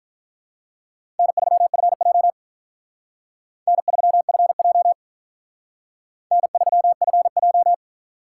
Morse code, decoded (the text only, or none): N3FJ N3FJ N3FJ